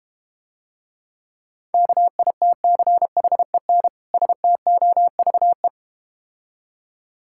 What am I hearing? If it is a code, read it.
Morse code, heard as a keyed tone, about 32 wpm, 700 Hz. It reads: KITCHEN STOVE